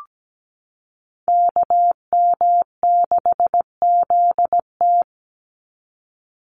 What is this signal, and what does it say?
Morse code, heard as a keyed tone, about 17 wpm, 700 Hz. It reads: KM6ZT
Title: Call Signs - 17wpm